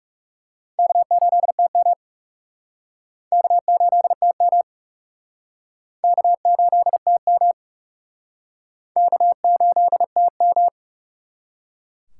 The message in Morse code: K8TM K8TM K8TM K8TM

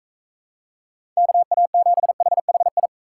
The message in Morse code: KA7HHI

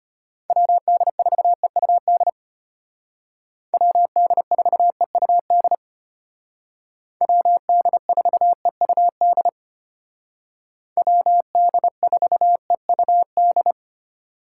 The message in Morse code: WD4EUD WD4EUD WD4EUD WD4EUD

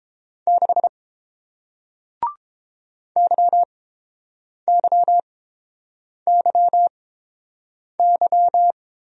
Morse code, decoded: 6 YYYY

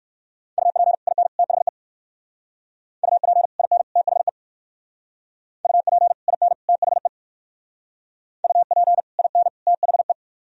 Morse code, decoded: UP IN THE UP IN THE UP IN THE UP IN THE

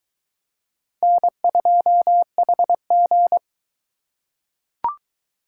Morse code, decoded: N2HG

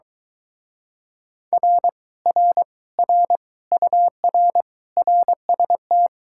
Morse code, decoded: R R R UR RST

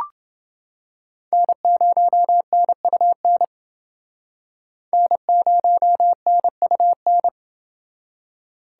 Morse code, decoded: N0NUN N0NUN

